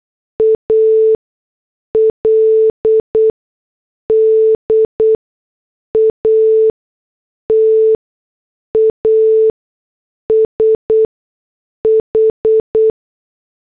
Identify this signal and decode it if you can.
Morse code, heard as a keyed tone, about 8 wpm, spaced as for 6 wpm, 440 Hz. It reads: AL DATA SH
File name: morse / headlines